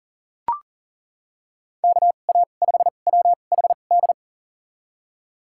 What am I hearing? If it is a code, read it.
Morse code, heard as a keyed tone, about 40 wpm, spaced as for 29 wpm, 700 Hz. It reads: KA5WHD
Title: USA Call Signs - 6 Characters - 2x Character Spacing - 40wpm